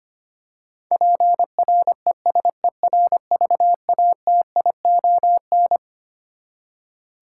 Morse code, heard as a keyed tone, about 25 wpm, 700 Hz. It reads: PRESERVATION
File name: Twelve Letter Words - 25wpm